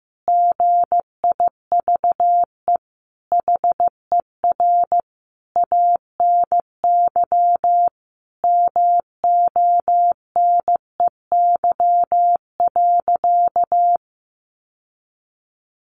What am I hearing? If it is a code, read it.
Morse code, heard as a keyed tone, about 14 wpm, 700 Hz. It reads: GIVE HER ANY MONEY.
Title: Sentences from Top 1500 Words 15wpm